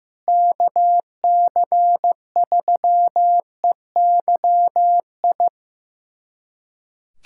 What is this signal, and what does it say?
Morse code, heard as a keyed tone, about 16 wpm, 700 Hz. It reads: KC3EYI
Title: USA Callsign DE Callsign - 6 Characters - 15wpm